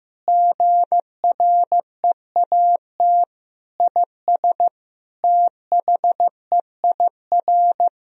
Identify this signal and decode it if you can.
Morse code, heard as a keyed tone, about 15 wpm, 700 Hz. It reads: GREAT IS THEIR
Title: Sentences from Top 1000 Words - Review Part 4 - 15wpm